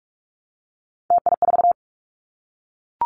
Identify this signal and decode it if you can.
Morse code, heard as a keyed tone, about 44 wpm, 700 Hz. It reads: TI4
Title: Callsign Prefixes for Top 100 DXCC - 45wpm